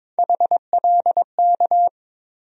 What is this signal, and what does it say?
Morse code, heard as a keyed tone, about 22 wpm, 700 Hz. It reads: HLK
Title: USA Call Sign Postfixes - 3 Letters 22wpm